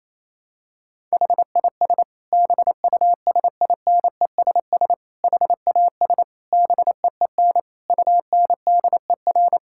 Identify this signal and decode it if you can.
Morse code, heard as a keyed tone, about 28 wpm, 700 Hz. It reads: HIS BUSINESS HAS BEEN UNDER
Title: Sentences from Top 1000 Words - Review Part 6 - 28wpm